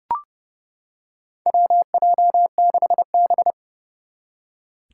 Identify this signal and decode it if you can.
Morse code, heard as a keyed tone, about 30 wpm, 700 Hz. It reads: WJ6B
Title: Four Challenging Letter-Number Combinations 30wpm